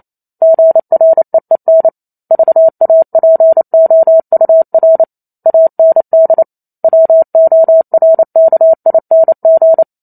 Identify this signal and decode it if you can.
Morse code, heard as a keyed tone, about 28 wpm, 650 Hz. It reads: GREEN VAPOUR AND WORKING